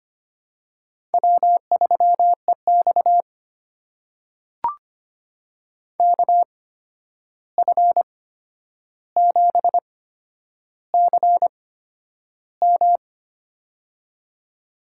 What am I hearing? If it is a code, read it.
Morse code, heard as a keyed tone, about 25 wpm, spaced as for 10 wpm, 700 Hz. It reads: W3EX KF7CM
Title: USA Call Signs - 8x Character Spacing - 25wpm